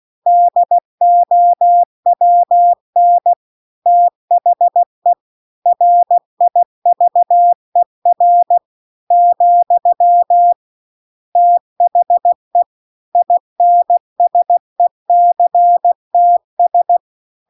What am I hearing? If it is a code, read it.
Morse code, heard as a keyed tone, about 16 wpm, 700 Hz. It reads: DOWN THE RIVER , THE INSECTS